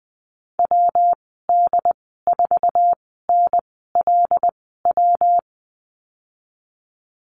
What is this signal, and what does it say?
Morse code, heard as a keyed tone, about 20 wpm, spaced as for 15 wpm, 700 Hz. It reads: WD4NLW
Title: 173 Lesson 28g - Number 9 - Call Signs 20wpm